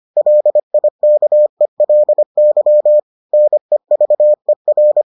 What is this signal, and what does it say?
Morse code, heard as a keyed tone, about 25 wpm, 600 Hz. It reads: LIKELY NEVER